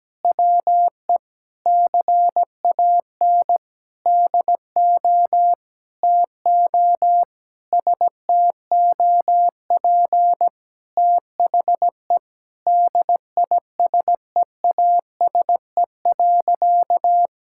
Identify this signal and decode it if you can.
Morse code, heard as a keyed tone, about 17 wpm, 700 Hz. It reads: WE CAN DO TO STOP THE DISEASE.
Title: Sentences from Top 1000 Words - Review Part 6 - 17wpm